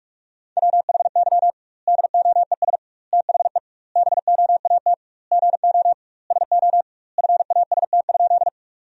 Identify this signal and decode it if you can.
Morse code, heard as a keyed tone, about 46 wpm, 700 Hz. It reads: WHY DOES THE BOAT GO SO FAST?
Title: 2x Word-Spacing - Sentences from Top 500 Words - 45wpm